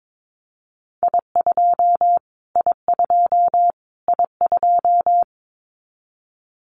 Morse code, heard as a keyed tone, about 22 wpm, 700 Hz. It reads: I2 I2 I2
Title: Callsign Prefixes for Top 25 DXCC - Repeated 3x - 22wpm